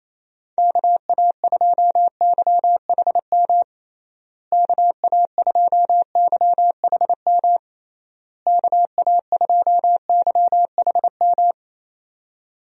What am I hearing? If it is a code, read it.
Morse code, heard as a keyed tone, about 28 wpm, 700 Hz. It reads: KA2YHM KA2YHM KA2YHM
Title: USA Call Signs - 6 Characters - Repeated 3x - 28wpm